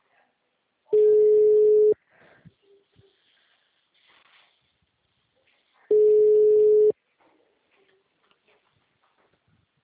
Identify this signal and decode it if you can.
Ringback tone (Europe, 425 Hz), with speech.